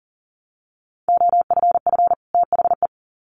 Morse code, heard as a keyed tone, about 40 wpm, 700 Hz. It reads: OFF THE
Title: Sets of 2 Words 40wpm